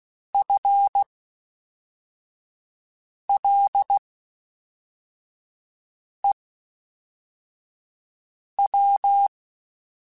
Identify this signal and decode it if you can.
Morse code, heard as a keyed tone, about 16 wpm, spaced as for 4 wpm, 800 Hz. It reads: FLEW
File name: Spaced / FL Runs sp mp3s / FL Runs sp @5